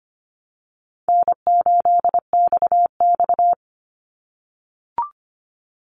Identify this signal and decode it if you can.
Morse code, heard as a keyed tone, about 25 wpm, 700 Hz. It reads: N8XX
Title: Call Signs - 25wpm